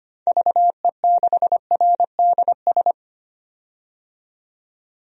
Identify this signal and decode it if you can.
Morse code, heard as a keyed tone, about 25 wpm, 700 Hz. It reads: VE6RDS
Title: DXCC-100 Call Signs - 6 Characters - 25wpm